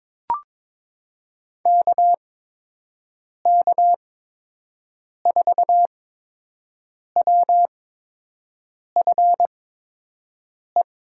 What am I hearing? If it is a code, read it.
Morse code, heard as a keyed tone, about 22 wpm, spaced as for 6 wpm, 700 Hz. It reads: KK4WFE
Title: USA Call Signs - 8x Character Spacing - 22wpm